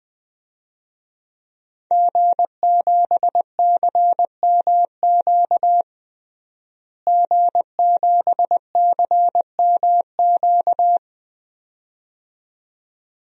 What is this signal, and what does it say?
Morse code, heard as a keyed tone, about 20 wpm, 700 Hz. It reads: G7CMQ G7CMQ
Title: DXCC-100 Call Signs - 5 Characters - Repeated 2x - 20wpm